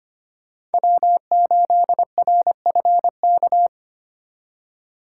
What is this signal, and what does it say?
Morse code, heard as a keyed tone, about 31 wpm, 700 Hz. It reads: W8RFK
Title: Speed Racing - USA 1x3 Call Signs 25wpm